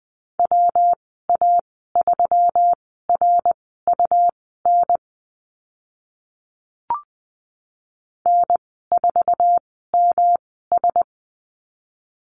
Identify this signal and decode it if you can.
Morse code, heard as a keyed tone, about 20 wpm, spaced as for 15 wpm, 700 Hz. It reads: WA3RUN N4MS
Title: 122 Lesson 20g - Number 3 - Call Signs 20wpm